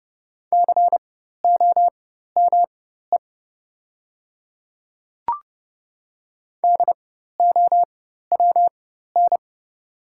Morse code, heard as a keyed tone, about 32 wpm, spaced as for 14 wpm, 700 Hz. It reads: COME DOWN
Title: 4x Character Spacing - Top 100 Words - 30wpm